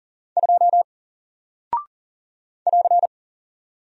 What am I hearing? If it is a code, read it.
Morse code, heard as a keyed tone, about 40 wpm, 700 Hz. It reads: 2 <AR>
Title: Single Character (Letter:Number:Punctuation:Prosign) Alt 40wpm